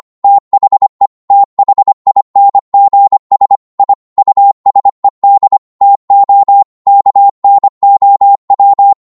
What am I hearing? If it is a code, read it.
Morse code, heard as a keyed tone, about 25 wpm, 800 Hz. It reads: THE THINGS I USED TO KNOW